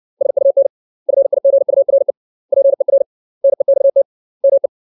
Morse code, heard as a keyed tone, about 60 wpm, 550 Hz. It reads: SUN FIGURE PER NEXT ME
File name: 632 words / 632 Words 60WPM